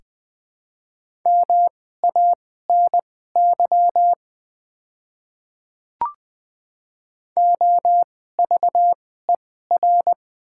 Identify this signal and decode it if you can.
Morse code, heard as a keyed tone, about 20 wpm, spaced as for 14 wpm, 700 Hz. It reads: MANY OVER
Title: 2x Character Spacing - Top 100 Words - 20wpm